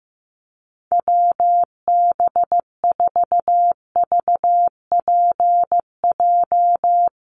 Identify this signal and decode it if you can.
Morse code, heard as a keyed tone, about 15 wpm, 700 Hz. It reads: WB4VPJ
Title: USA Call Signs Repeated 2x 15wpm